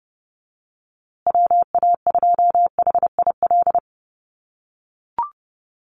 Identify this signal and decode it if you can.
Morse code, heard as a keyed tone, about 30 wpm, 700 Hz. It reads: WA2HIL
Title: USA Call Signs Repeated 3x 30wpm